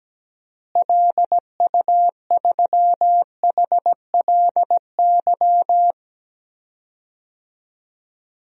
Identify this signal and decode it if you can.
Morse code, heard as a keyed tone, about 17 wpm, 700 Hz. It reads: LU3HLY
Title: DXCC-100 Call Signs - 6 Characters - Repeated 3x - 17wpm